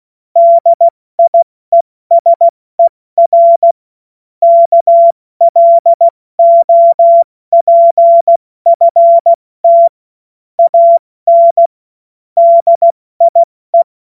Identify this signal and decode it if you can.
Morse code, heard as a keyed tone, about 16 wpm, spaced as for 14 wpm, 680 Hz. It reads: DIESER KLOPFT AN DIE
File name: Morsekurs / Witze / Witz 26. Oktober 2021, 70 F80